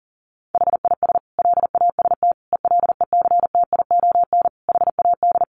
Morse code, heard as a keyed tone, about 40 wpm, 700 Hz. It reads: HIS LAST ELECTION HAD
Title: Sentences from Top 1500 Words 40wpm